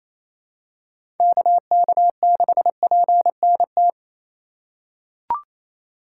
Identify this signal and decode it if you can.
Morse code, heard as a keyed tone, about 28 wpm, 700 Hz. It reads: KK6PNT
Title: USA Call Signs - 3x Character Spacing - 28wpm